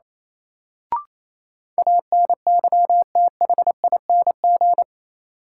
Morse code, heard as a keyed tone, about 28 wpm, 700 Hz. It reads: ANYTHING
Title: Sentences from Top 1000 Words - New Word Drill - 28wpm